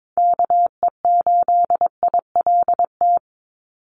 Morse code, heard as a keyed tone, about 22 wpm, 700 Hz. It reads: KE8ILT
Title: USA Call Signs 22wpm